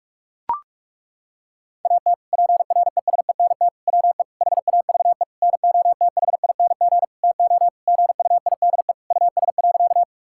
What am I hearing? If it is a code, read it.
Morse code, heard as a keyed tone, about 41 wpm, 700 Hz. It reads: AT PRESENT WE HAVE NOTHING TO GUIDE US.
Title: Sentences from Top 1000 Words - Review Part 5 - 45wpm